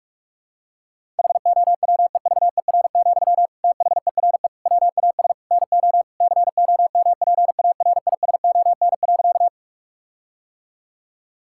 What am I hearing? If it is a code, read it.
Morse code, heard as a keyed tone, about 46 wpm, 700 Hz. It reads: HOWEVER, THERE WAS NO COMPARISON.
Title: Sentences from Top 1600 Words 45wpm